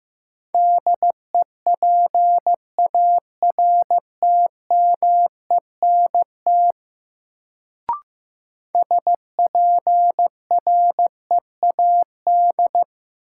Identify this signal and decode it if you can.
Morse code, heard as a keyed tone, about 15 wpm, 700 Hz. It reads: DEPARTMENT SPREAD
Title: Sentences from Top 1200 Words - New Word Drill Alt - 15wpm